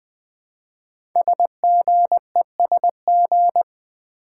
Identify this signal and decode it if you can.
Morse code, heard as a keyed tone, about 20 wpm, 700 Hz. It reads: SGESG
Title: Code Groups - 5 Letters - 20wpm